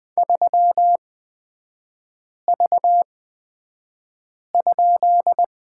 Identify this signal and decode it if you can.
Morse code, heard as a keyed tone, about 20 wpm, spaced as for 5 wpm, 700 Hz. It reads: 3V?
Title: 250 Lesson 40e - BK - Three Characters 20wpm